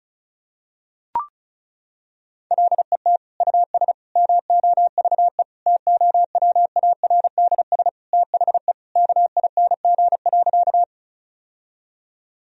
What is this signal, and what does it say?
Morse code, heard as a keyed tone, about 36 wpm, 700 Hz. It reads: LET US MOVE TOWARDS THE KING.
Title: Sentences from Top 1000 Words - Review Part 2 - 35wpm